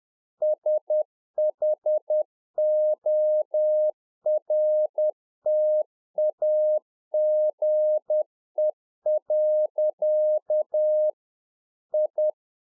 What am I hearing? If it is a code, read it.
Morse code, heard as a keyed tone, about 10 wpm, 610 Hz. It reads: SHORTAGE. I
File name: 2011-04-mp3 / A SMALL TRANCEIVER 1 10WPM